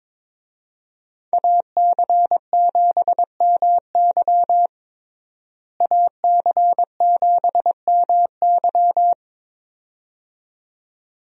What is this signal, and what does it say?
Morse code, heard as a keyed tone, about 21 wpm, 700 Hz. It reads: AC7MY AC7MY
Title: USA Call Signs - 5 Characters - Repeated 2x - 22wpm